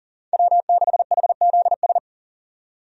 Morse code, heard as a keyed tone, about 38 wpm, 700 Hz. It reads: W6HZS
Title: USA Call Signs - 5 Characters - 40wpm